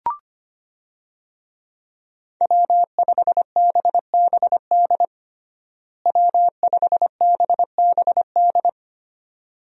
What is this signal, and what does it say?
Morse code, heard as a keyed tone, about 25 wpm, 700 Hz. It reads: W5BBD W5BBD
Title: USA Call Signs - 5 Characters - Repeated 2x - 25wpm